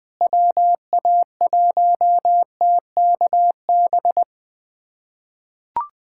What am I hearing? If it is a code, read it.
Morse code, heard as a keyed tone, about 20 wpm, 700 Hz. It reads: WA1TKB
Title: USA Callsign DE Callsign - 6 Characters - 20wpm